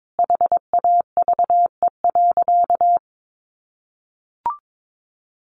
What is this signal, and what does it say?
Morse code, heard as a keyed tone, about 22 wpm, 700 Hz. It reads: HAVE.
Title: Sentences from Top 1100 Words 22wpm